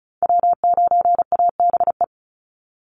Morse code, heard as a keyed tone, about 34 wpm, 700 Hz. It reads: W9ABE
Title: USA Call Signs 35wpm